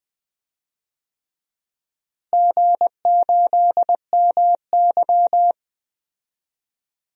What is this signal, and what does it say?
Morse code, heard as a keyed tone, about 20 wpm, 700 Hz. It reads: G8MY
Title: DXCC-100 Call Signs - 4 Characters - Repeated 3x - 20wpm